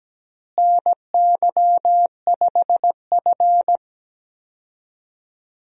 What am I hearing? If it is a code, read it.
Morse code, heard as a keyed tone, about 17 wpm, 700 Hz. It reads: NY5F
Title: USA Call Signs - 4 Characters - Repeated 3x - 17wpm